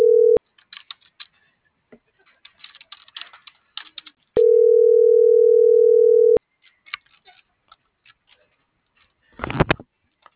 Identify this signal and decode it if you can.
Ringback tone (North America), 440 + 480 Hz.